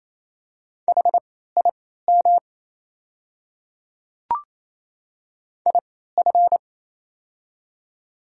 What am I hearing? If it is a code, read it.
Morse code, heard as a keyed tone, about 28 wpm, spaced as for 16 wpm, 700 Hz. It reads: HIM IF
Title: 3x Character Spacing - Top 100 Words - 28wpm